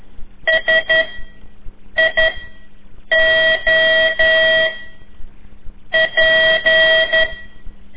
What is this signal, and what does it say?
Morse code, heard as a keyed tone, about 12 wpm, 1950 Hz. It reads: S I O P
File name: ik1weg2 / CORSI CW / lez4-3